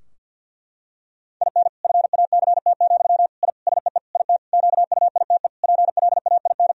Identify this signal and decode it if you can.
Morse code, heard as a keyed tone, about 50 wpm, 700 Hz. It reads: IN FACT, I SEE IT QUITE PLAIN
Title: Sentences from Top 1000 Words - Review Part 3 - 50wpm